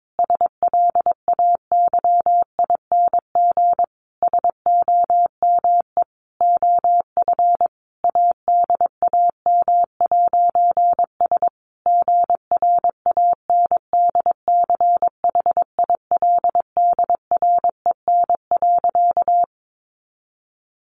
Morse code, heard as a keyed tone, about 22 wpm, 700 Hz. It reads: SLAYING SOME OF ADAM'S GRANDCHILDREN.
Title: A Sound Of Thunder 22wpm